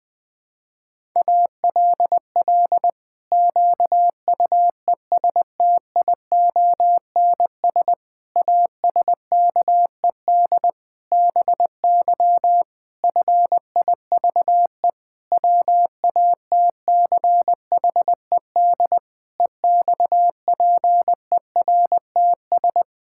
Morse code, heard as a keyed tone, about 19 wpm, 700 Hz. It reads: ALL QUESTIONS ASKED BY FIVE WATCHED EXPERTS
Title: Pangram Challenge v2 - 20wpm